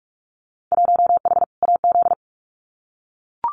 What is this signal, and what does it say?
Morse code, heard as a keyed tone, about 44 wpm, 700 Hz. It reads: 1H AZ
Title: Winter Field Day Exchanges 45wpm